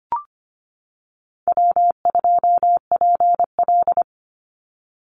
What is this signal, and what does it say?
Morse code, heard as a keyed tone, about 25 wpm, 700 Hz. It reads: W2PL
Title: Easy Call Signs - 25wpm